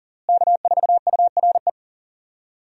Morse code, heard as a keyed tone, about 40 wpm, 700 Hz. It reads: K4URE
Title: USA Call Signs - 5 Characters - Repeated 3x - 40wpm